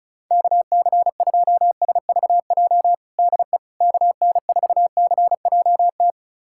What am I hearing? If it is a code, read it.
Morse code, heard as a keyed tone, about 36 wpm, 690 Hz. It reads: KC2SVJ DE KN4CJT
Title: USA Callsign DE Callsign - 6 Characters - 35wpm